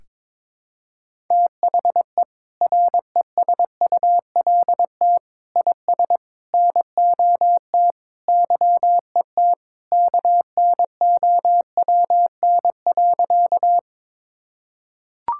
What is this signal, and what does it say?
Morse code, heard as a keyed tone, about 22 wpm, 700 Hz. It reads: THE RESULT IS NOT YET KNOWN.
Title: Sentences from Top 1000 Words - Review Part 5 - 22wpm